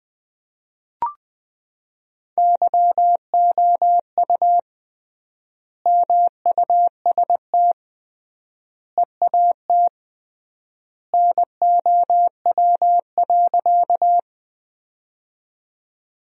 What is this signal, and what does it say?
Morse code, heard as a keyed tone, about 20 wpm, 700 Hz. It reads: YOU MUST EAT NOW.
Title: 3x Word-Spacing - Sentences from Top 300 Words - 20wpm